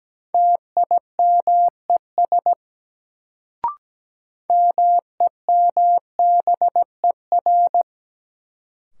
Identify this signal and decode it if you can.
Morse code, heard as a keyed tone, about 17 wpm, 700 Hz. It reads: TIMES MEMBER
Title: Sentences from Top 1100 Words - New Word Drill Alt - 17wpm